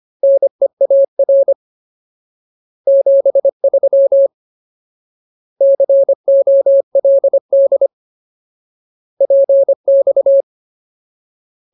Morse code, heard as a keyed tone, about 25 wpm, 550 Hz. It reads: NEAR 73 COLD PX
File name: Ultimate Word List / 25WPM Ultimate